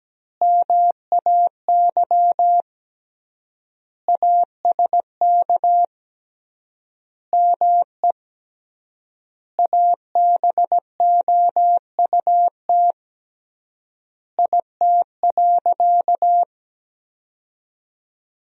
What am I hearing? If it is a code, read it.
Morse code, heard as a keyed tone, about 17 wpm, 700 Hz. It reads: MAY ASK ME ABOUT IT.
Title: 3x Word-Spacing - Sentences from Top 200 Words - 17wpm